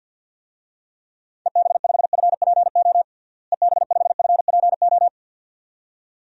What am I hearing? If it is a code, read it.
Morse code, heard as a keyed tone, about 46 wpm, 700 Hz. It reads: EB5FPO EB5FPO
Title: DXCC-100 Call Signs - 6 Characters - Repeated 2x - 50wpm